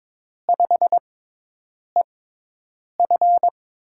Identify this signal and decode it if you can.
Morse code, heard as a keyed tone, about 22 wpm, spaced as for 8 wpm, 700 Hz. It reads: 5EF
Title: USA Call Signs - 4 Characters - 6x Character Spacing - 22wpm